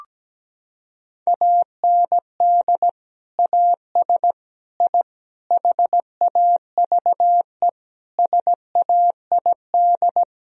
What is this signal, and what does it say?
Morse code, heard as a keyed tone, about 17 wpm, 700 Hz. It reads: AND AS I HAVE SAID
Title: Sentences from Top 1000 Words - Review Part 2 - 17wpm